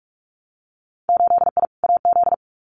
Morse code, heard as a keyed tone, about 46 wpm, 700 Hz. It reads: 8I AZ
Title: Winter Field Day Exchanges Encore 45wpm